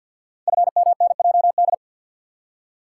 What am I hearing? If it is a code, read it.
Morse code, heard as a keyed tone, about 51 wpm, 700 Hz. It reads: FMNJD
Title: Code Groups - 5 Letters - 50wpm